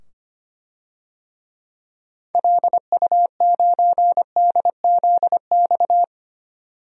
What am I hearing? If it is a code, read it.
Morse code, heard as a keyed tone, about 25 wpm, 700 Hz. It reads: LU9DZX
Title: DXCC-100 Call Signs - 6 Characters - 25wpm